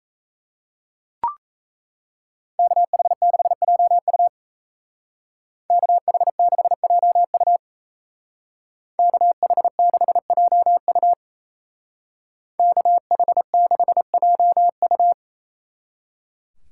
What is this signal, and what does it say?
Morse code, heard as a keyed tone, about 34 wpm, 700 Hz. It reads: KH6JU KH6JU KH6JU KH6JU
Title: Speed Racing - USA 2x2 Call Signs 28wpm